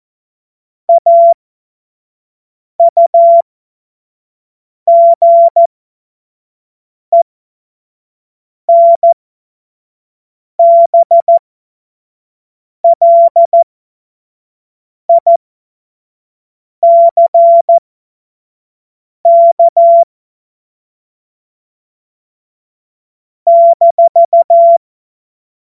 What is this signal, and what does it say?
Morse code, heard as a keyed tone, about 14 wpm, spaced as for 5 wpm, 680 Hz. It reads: AUGENBLICK -
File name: Morsekurs / Witze / Witz 25. Oktober 2022 25 F70